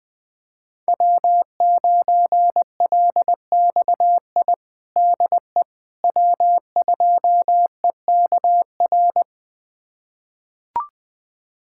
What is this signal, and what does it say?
Morse code, heard as a keyed tone, about 20 wpm, 700 Hz. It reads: W9LXI DE W2EKR
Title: USA Callsign DE Callsign - 5 Characters - 20wpm